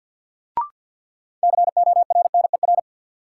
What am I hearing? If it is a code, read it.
Morse code, heard as a keyed tone, about 54 wpm, 700 Hz. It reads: CORNER
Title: Sets of 5 Words - New Word Drill - 50wpm